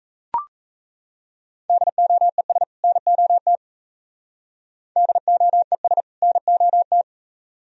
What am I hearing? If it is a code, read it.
Morse code, heard as a keyed tone, about 34 wpm, 700 Hz. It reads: DOES NOT DOES NOT
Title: Speed-Racing - Sets of 2 Words 28wpm